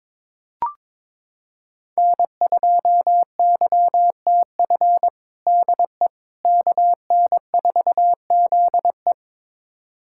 Morse code, heard as a keyed tone, about 22 wpm, 700 Hz. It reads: N2YTF DE KN4ZE
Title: USA Callsign DE Callsign - 5 Characters - 22wpm